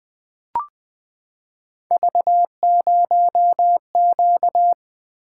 Morse code, heard as a keyed tone, about 20 wpm, 700 Hz. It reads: V0Q
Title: Three Letter-Number Combinations - 20wpm